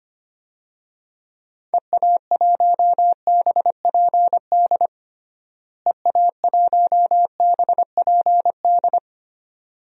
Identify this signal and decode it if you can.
Morse code, heard as a keyed tone, about 25 wpm, 700 Hz. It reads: EA1BPD EA1BPD
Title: DXCC-100 Call Signs - 6 Characters - Repeated 2x - 25wpm